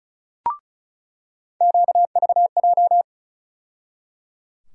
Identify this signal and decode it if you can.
Morse code, heard as a keyed tone, about 34 wpm, 700 Hz. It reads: QVJ
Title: USA Call Sign Postfixes - 3 Letters 35wpm